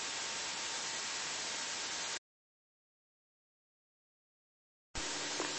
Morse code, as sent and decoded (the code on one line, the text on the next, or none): none